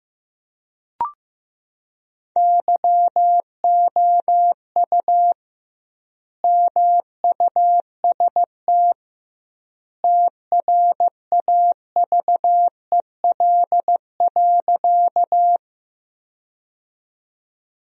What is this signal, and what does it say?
Morse code, heard as a keyed tone, about 15 wpm, 700 Hz. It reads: YOU MUST TRAVEL.
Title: 2x Word-Spacing - Sentences from Top 400 Words - 15wpm